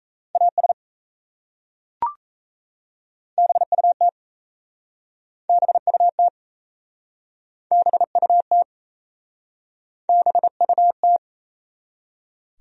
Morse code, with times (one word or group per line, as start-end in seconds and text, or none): 0.4-0.7 AS
3.4-4.1 BUT
5.5-6.3 BUT
7.7-8.6 BUT
10.1-11.2 BUT